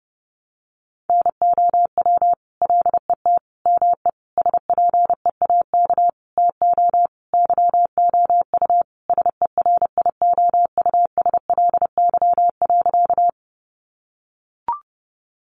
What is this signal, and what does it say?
Morse code, heard as a keyed tone, about 30 wpm, 700 Hz. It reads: NOW LET ME SPEAK TO YOU SERIOUSLY.
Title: Sentences from Top 1200 Words 30wpm